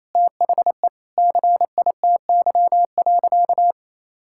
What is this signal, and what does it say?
Morse code, heard as a keyed tone, about 28 wpm, 700 Hz. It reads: THE CITY.
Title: Sentences from Top 1000 Words - Review Part 6 - 28wpm